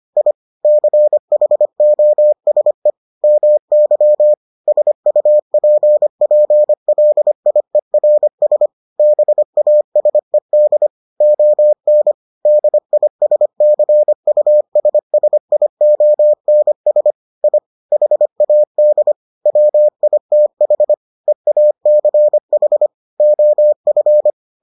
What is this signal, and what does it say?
Morse code, heard as a keyed tone, about 25 wpm, 600 Hz. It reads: I CHOSE MY SUPPLIERS BASED ON DISCUSSIONS I HAD WITH EACH OF